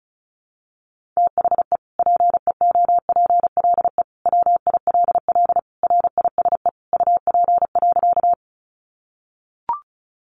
Morse code, heard as a keyed tone, about 34 wpm, 700 Hz. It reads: THE PEOPLE WILL RISE UP.
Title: Sentences from Top 900 Words 35wpm